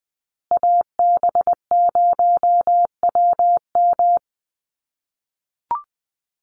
Morse code, heard as a keyed tone, about 20 wpm, 700 Hz. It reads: AB0WM
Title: USA Call Signs Repeated 2x 20wpm